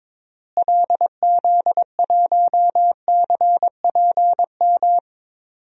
Morse code, heard as a keyed tone, about 22 wpm, 700 Hz. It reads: LZ1CPM
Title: DXCC-100 Call Signs - 6 Characters - Repeated 3x - 22wpm